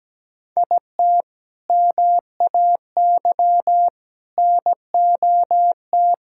Morse code, heard as a keyed tone, about 17 wpm, 700 Hz. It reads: IT MAY NOT
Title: Sentences from Top 1000 Words - Review Part 6 - 17wpm